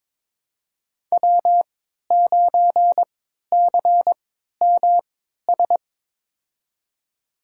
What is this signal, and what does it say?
Morse code, heard as a keyed tone, about 22 wpm, spaced as for 12 wpm, 700 Hz. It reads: W9CMS
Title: USA Call Signs - 5 Characters - 3x Character Spacing - 22wpm